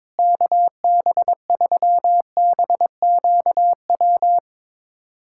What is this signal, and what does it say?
Morse code, heard as a keyed tone, about 22 wpm, 700 Hz. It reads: KB3BQW
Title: USA Callsign DE Callsign - 6 Characters - 22wpm